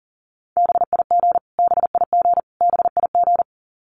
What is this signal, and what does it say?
Morse code, heard as a keyed tone, about 40 wpm, 700 Hz. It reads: BIG BIG BIG
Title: Top 200 Words Repeated 40wpm